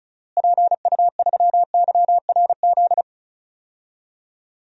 Morse code, heard as a keyed tone, about 36 wpm, 700 Hz. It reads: PU3YRZ
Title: DXCC-100 Call Signs - 6 Characters - Repeated 3x - 35wpm